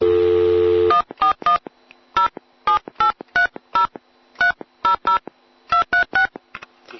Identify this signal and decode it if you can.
DTMF, heard as telephone touch tones, tones of about 110 ms. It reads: 141 07830300236